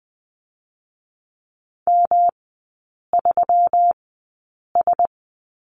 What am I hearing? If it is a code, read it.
Morse code, heard as a keyed tone, about 20 wpm, spaced as for 8 wpm, 700 Hz. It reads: M3S
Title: 121 Lesson 20e - Number 3 - Three Characters 20wpm